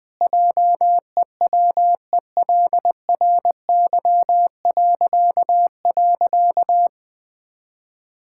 Morse code, heard as a keyed tone, about 20 wpm, 700 Hz. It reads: JEWELRY..
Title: Pangram Challenge v2 - 20wpm